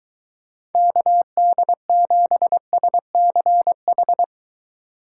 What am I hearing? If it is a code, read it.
Morse code, heard as a keyed tone, about 23 wpm, 700 Hz. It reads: KD7SCH